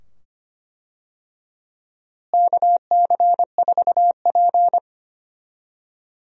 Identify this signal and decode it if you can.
Morse code, heard as a keyed tone, about 25 wpm, 700 Hz. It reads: KC4P
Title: USA Call Signs - 4 Characters - Repeated 2x - 25wpm